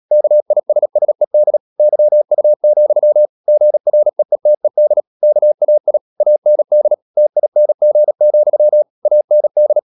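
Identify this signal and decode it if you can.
Morse code, heard as a keyed tone, about 37 wpm, 600 Hz. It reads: KISSED YU, GREETED KAI AND TING, AND
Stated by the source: Morse code practice at 35–39 wpm sending English text of Ken Follett's novel Never, from CW med Gnister.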